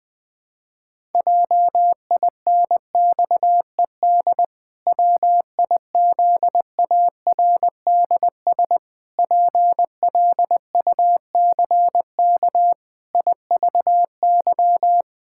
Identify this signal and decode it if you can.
Morse code, heard as a keyed tone, about 21 wpm, 700 Hz. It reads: JINXED WIZARDS PLUCK IVY
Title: Pangram Challenge v2 - 20wpm